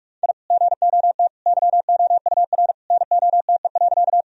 Morse code, heard as a keyed tone, about 45 wpm, 700 Hz. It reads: I GOT YOUR NOTE.
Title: Sentences from Top 1000 Words - Review Part 2 - 45wpm